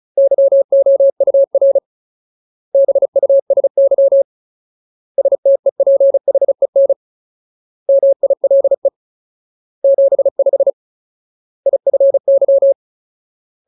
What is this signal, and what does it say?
Morse code, heard as a keyed tone, about 37 wpm, 550 Hz. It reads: YOUR BUSY STEPHEN MILE 75 IFY
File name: Ultimate Word List / 35WPM Ultimate